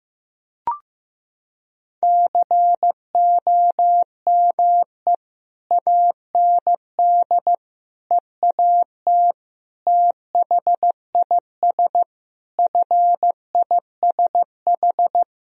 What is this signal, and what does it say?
Morse code, heard as a keyed tone, about 15 wpm, 700 Hz. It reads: COME AND EAT THIS FISH
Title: Sentences from Top 1000 Words - Review Part 1 - 15wpm